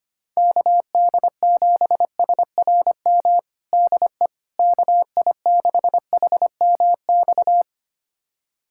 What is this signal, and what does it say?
Morse code, heard as a keyed tone, about 25 wpm, 700 Hz. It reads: KD7SRM DE KI6HMX
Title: USA Callsign DE Callsign 25wpm